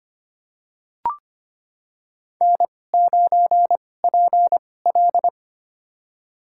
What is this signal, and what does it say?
Morse code, heard as a keyed tone, about 25 wpm, spaced as for 18 wpm, 700 Hz. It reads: N9PL
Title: USA Call Signs - 4 Characters - 2x Character Spacing - 25wpm